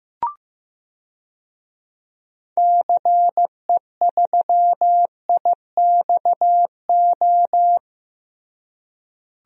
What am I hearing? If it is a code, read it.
Morse code, heard as a keyed tone, about 15 wpm, 700 Hz. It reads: CE3IXO
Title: DXCC-100 Call Signs - 6 Characters - Repeated 2x - 15wpm